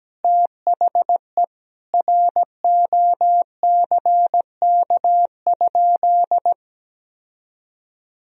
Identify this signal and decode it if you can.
Morse code, heard as a keyed tone, about 17 wpm, 700 Hz. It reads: THE ROCK?
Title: Sentences from Top 1000 Words - Review Part 2 - 17wpm